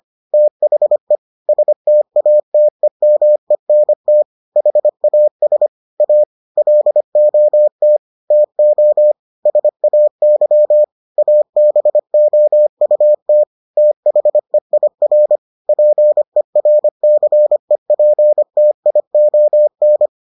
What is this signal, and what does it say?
Morse code, heard as a keyed tone, about 25 wpm, 600 Hz. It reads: THE STATEMENT HAS A LOT TO SAY ABOUT THEIR PERCEPTION